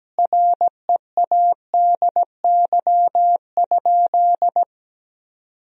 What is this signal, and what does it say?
Morse code, heard as a keyed tone, about 17 wpm, 700 Hz. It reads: READY?
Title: Sentences from Top 400 Words and Abbreviations 17wpm